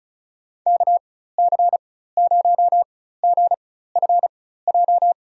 Morse code, heard as a keyed tone, about 34 wpm, spaced as for 16 wpm, 700 Hz. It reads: KC0GFJ
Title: USA Call Signs - 6 Characters - 4x Character Spacing - 35wpm